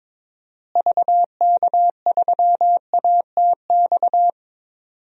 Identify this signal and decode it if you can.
Morse code, heard as a keyed tone, about 22 wpm, 700 Hz. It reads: VK3ATX
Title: DXCC-100 Call Signs - 6 Characters - Repeated 3x - 22wpm